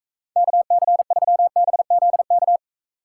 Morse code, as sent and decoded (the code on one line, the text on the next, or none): -.- -.-. ...-- -... --.. -.-
KC3BZK